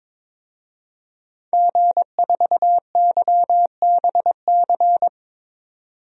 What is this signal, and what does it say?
Morse code, heard as a keyed tone, about 22 wpm, 700 Hz. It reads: G4YBC
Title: DXCC-100 Call Signs - 5 Characters - 22wpm